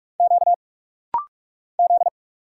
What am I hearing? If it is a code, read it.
Morse code, heard as a keyed tone, about 44 wpm, 700 Hz. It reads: Q Z
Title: Single Letters - Rapid-Fire 45wpm